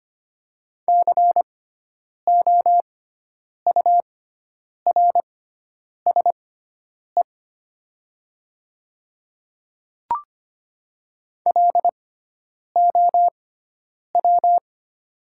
Morse code, heard as a keyed tone, about 25 wpm, spaced as for 9 wpm, 700 Hz. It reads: COURSE LOW
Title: 6x Character Spacing - Top 500 Words - 25wpm